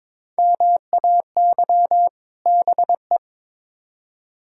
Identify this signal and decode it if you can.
Morse code, heard as a keyed tone, about 22 wpm, 700 Hz. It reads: MAY BE
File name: Build A Phrase - 2-Word Phrases - 22wpm